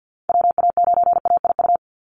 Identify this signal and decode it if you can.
Morse code, heard as a keyed tone, about 48 wpm, 700 Hz. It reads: WA9AIU
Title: USA Call Signs 50wpm